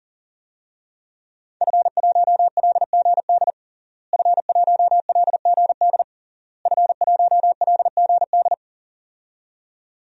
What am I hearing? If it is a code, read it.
Morse code, heard as a keyed tone, about 40 wpm, 700 Hz. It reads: F1LGD F1LGD F1LGD
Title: DXCC-100 Call Signs - 5 Characters - Repeated 3x - 40wpm